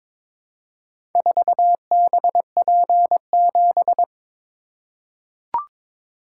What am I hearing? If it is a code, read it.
Morse code, heard as a keyed tone, about 22 wpm, 700 Hz. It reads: 4BP7
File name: Four Letter-Number Combinations - 22wpm